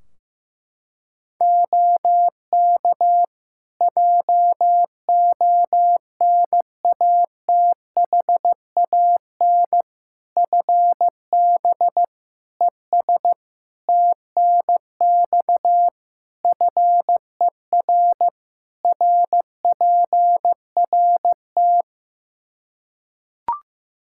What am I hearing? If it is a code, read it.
Morse code, heard as a keyed tone, about 15 wpm, 700 Hz. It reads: OK JONATHAN FB ES TNX FER RPRT
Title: Practice QSO - Exchange 2 - Info - 15wpm